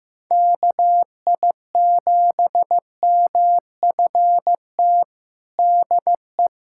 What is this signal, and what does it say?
Morse code, heard as a keyed tone, about 16 wpm, 700 Hz. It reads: KI7MFT DE
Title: USA Callsign DE Callsign - 6 Characters - 15wpm